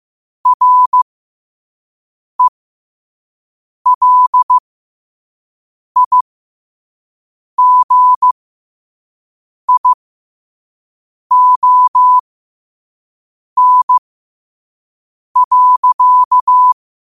Morse code, heard as a keyed tone, about 15 wpm, spaced as for 5 wpm, 1000 Hz. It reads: RELIGION.
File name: qotdfiles / 5wpm